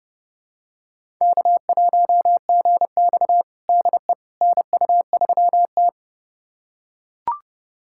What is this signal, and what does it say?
Morse code, heard as a keyed tone, about 30 wpm, 700 Hz. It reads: K1GX DE NU3T
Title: USA Callsign DE Callsign - 4 Characters - 30wpm